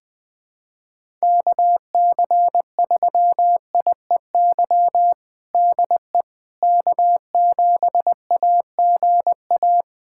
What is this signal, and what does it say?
Morse code, heard as a keyed tone, about 20 wpm, 700 Hz. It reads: KC3IEY DE K7AGA
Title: USA Callsign DE Callsign 20wpm